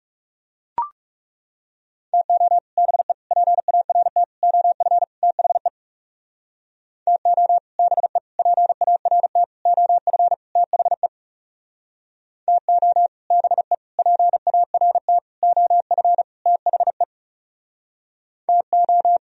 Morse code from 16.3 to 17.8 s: THE

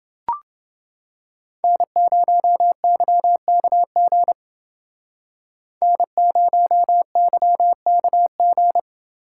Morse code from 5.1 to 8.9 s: N0YKG

